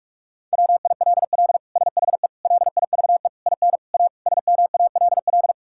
Morse code, heard as a keyed tone, about 46 wpm, 700 Hz. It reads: WILL SHE LIVE IN A SMALL
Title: Sentences from Top 1000 Words - Review Part 6 - 45wpm